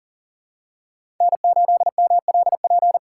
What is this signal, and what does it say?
Morse code, heard as a keyed tone, about 40 wpm, 700 Hz. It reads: N8MLP
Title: USA Call Signs - 5 Characters - 2x Character Spacing - 40wpm